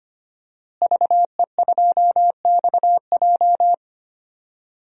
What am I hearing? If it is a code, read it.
Morse code, heard as a keyed tone, about 25 wpm, 700 Hz. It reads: VE2XJ
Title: DXCC-100 Call Signs - 5 Characters - Repeated 3x - 25wpm